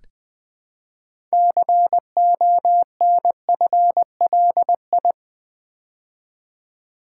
Morse code, heard as a keyed tone, about 20 wpm, 700 Hz. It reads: CONFLI
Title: Build A Word - 3 Letters At A Time - 8-Letter Words - 20wpm